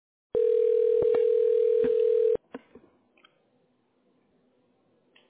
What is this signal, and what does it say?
Ringback tone (North America), 440 + 480 Hz.